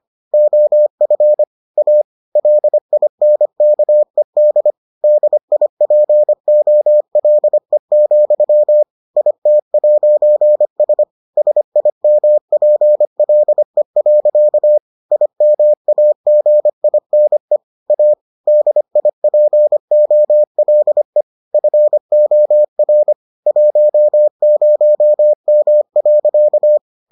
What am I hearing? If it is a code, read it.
Morse code, heard as a keyed tone, about 25 wpm, 600 Hz. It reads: OF A LINKED DIPOLE, IT'S SIMPLE. IMAGINE A DIPOLE FOR 10M.